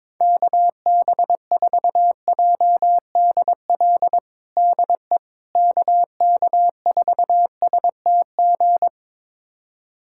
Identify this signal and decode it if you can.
Morse code, heard as a keyed tone, about 21 wpm, 700 Hz. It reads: KB4JDL DE KK4STG
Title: USA Callsign DE Callsign 22wpm